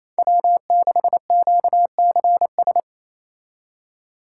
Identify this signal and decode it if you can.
Morse code, heard as a keyed tone, about 35 wpm, 700 Hz. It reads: W6QCS
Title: Speed Racing - USA 1x3 Call Signs 28wpm